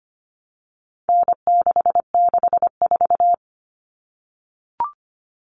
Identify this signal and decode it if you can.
Morse code, heard as a keyed tone, about 25 wpm, 700 Hz. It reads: N664
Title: All Cut Numbers - 4-Digit 25wpm